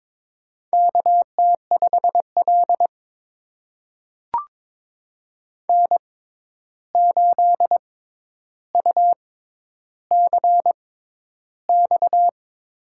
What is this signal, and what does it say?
Morse code, heard as a keyed tone, about 22 wpm, spaced as for 11 wpm, 700 Hz. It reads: KT5L N8UCX
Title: USA Call Signs - 6x Character Spacing - 22wpm